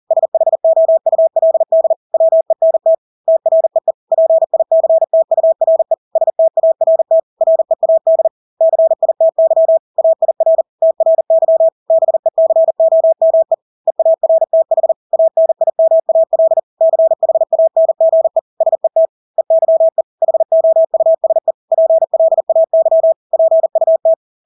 Morse code, heard as a keyed tone, about 40 wpm, 650 Hz. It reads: SHOULD WENT TREE PICTURE START READ CITY AIR TRY BECOME EARTH ANIMAL CHANGE SET EYE HOUSE PLAY PUT
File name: Engelske ord 40-40 wpm